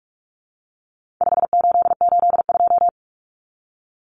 Morse code, heard as a keyed tone, about 44 wpm, 700 Hz. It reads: 5882
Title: Numbers - 4-Digit 45wpm